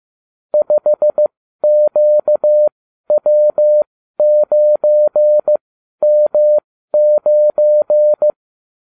Morse code, heard as a keyed tone, about 15 wpm, spaced as for 12 wpm, 600 Hz. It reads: 5QW9M9
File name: cw / ubacw29 2